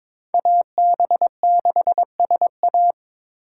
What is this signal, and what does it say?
Morse code, heard as a keyed tone, about 22 wpm, 700 Hz. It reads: AB6SA
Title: USA Call Signs - 5 Characters - 6x Character Spacing - 22wpm